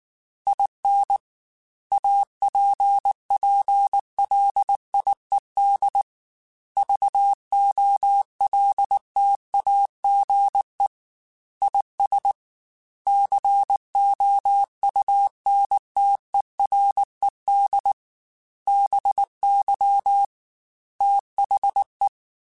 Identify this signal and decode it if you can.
Morse code, heard as a keyed tone, about 19 wpm, 800 Hz. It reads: IN APPLIED VOLTAGE IS COUNTERED BY THE
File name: Technical Subjects / Technical Subjects-mp3s / XC @18